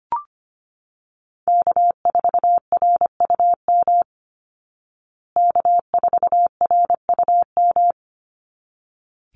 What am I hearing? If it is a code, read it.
Morse code, heard as a keyed tone, about 25 wpm, 700 Hz. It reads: K4RUM K4RUM
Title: Call Signs Repeated 3x - 25wpm